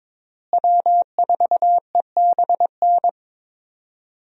Morse code, heard as a keyed tone, about 22 wpm, 700 Hz. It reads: W4EBN
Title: USA Call Signs - 5 Characters - Repeated 2x - 22wpm